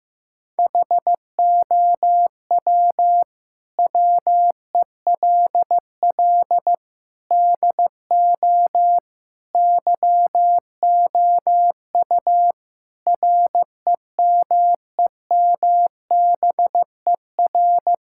Morse code, heard as a keyed tone, about 15 wpm, 700 Hz. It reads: HOW WELL DO YOU REMEMBER
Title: Sentences from Top 1000 Words - Review Part 5 - 15wpm